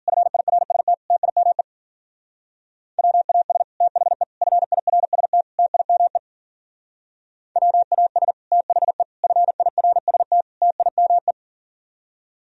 Morse code, heard as a keyed tone, about 52 wpm, 700 Hz. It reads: FIRST TIME WAS THE FIRST TIME WAS THE FIRST TIME